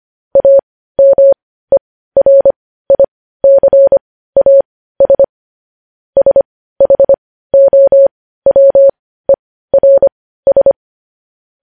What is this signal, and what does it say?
Morse code, heard as a keyed tone, about 25 wpm, spaced as for 15 wpm, 550 Hz. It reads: AMERICAS SHOWERS
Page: news-headlines-in-morse-code-at-15-wpm